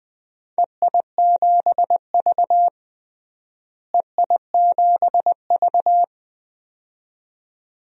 Morse code, heard as a keyed tone, about 21 wpm, 700 Hz. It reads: EI7V EI7V
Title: DXCC-100 Call Signs - 4 Characters - Repeated 2x - 20wpm